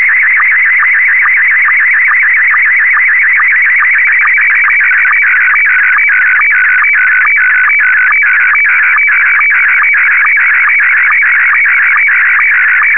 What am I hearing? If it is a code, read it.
SSTV (slow-scan television), Scottie 1.
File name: ham / sstv test-1